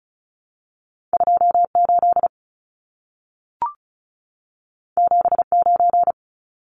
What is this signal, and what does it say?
Morse code, heard as a keyed tone, about 36 wpm, 700 Hz. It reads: 28 79
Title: Numbers - 2-Digit 35wpm